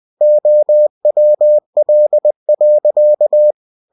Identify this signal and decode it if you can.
Morse code, heard as a keyed tone, about 20 wpm, 600 Hz. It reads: OWL.